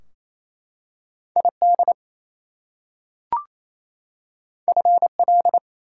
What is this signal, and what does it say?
Morse code, heard as a keyed tone, about 28 wpm, 700 Hz. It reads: ID FL
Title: US States and Canadian Provinces 28wpm